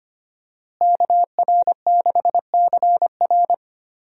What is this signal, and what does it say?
Morse code, heard as a keyed tone, about 25 wpm, 700 Hz. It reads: KR6CR
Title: USA Call Signs - 5 Characters - Repeated 3x - 25wpm